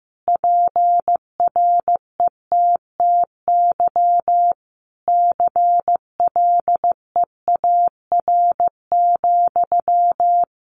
Morse code, heard as a keyed tone, about 15 wpm, 700 Hz. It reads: PRETTY CLEAR,
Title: Sentences from Top 1000 Words 15wpm